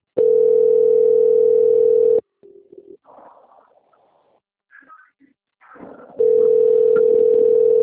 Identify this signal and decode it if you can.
Ringback tone (North America), 440 + 480 Hz.
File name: out- - - - - .45297